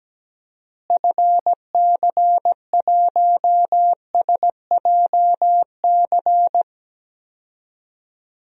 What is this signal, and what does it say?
Morse code, heard as a keyed tone, about 17 wpm, 700 Hz. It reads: FC1SJC
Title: DXCC-100 Call Signs - 6 Characters - Repeated 3x - 17wpm